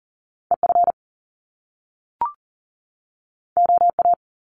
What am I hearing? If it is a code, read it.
Morse code, heard as a keyed tone, about 38 wpm, 700 Hz. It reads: EF OA
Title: Top Two Letter Combinations in Words 40wpm